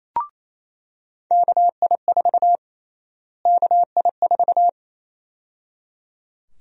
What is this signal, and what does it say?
Morse code, heard as a keyed tone, about 28 wpm, 700 Hz. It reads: KI4 KI4
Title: USA Call Sign Prefixes - Repeated 2x - 28wpm